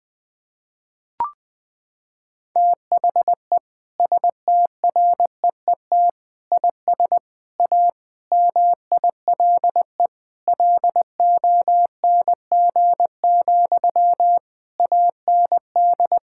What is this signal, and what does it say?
Morse code, heard as a keyed tone, about 20 wpm, 700 Hz. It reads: THE STREET IS A MILE LONG, AND
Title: Sentences from Top 1600 Words 20wpm